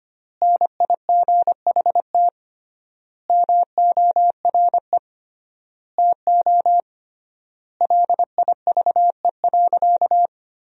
Morse code, heard as a keyed tone, about 25 wpm, 700 Hz. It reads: NIGHT MORE TO LIVE.
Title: 3x Word-Spacing - Sentences from Top 300 Words - 25wpm